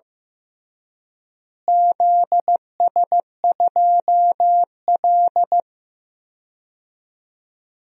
Morse code, heard as a keyed tone, about 15 wpm, 700 Hz. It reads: ZS2L
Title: DXCC-100 Call Signs - 4 Characters - Repeated 3x - 15wpm